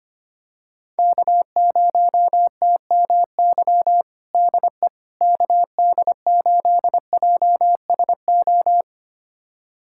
Morse code, heard as a keyed tone, about 25 wpm, 700 Hz. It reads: K0TMY DE KD8JSO
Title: USA Callsign DE Callsign 25wpm